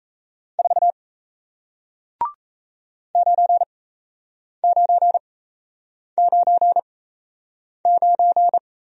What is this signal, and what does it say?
Morse code, heard as a keyed tone, about 37 wpm, spaced as for 8 wpm, 700 Hz. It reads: 4 9999